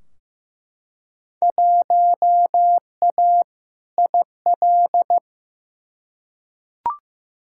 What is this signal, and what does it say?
Morse code, heard as a keyed tone, about 15 wpm, 700 Hz. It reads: 1A IL
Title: Field Day Exchanges Encore - 15wpm